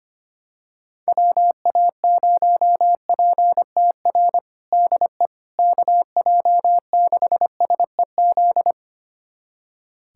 Morse code, heard as a keyed tone, about 25 wpm, 700 Hz. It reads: WA0PTR DE KJ6SEZ
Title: USA Callsign DE Callsign - 6 Characters - 25wpm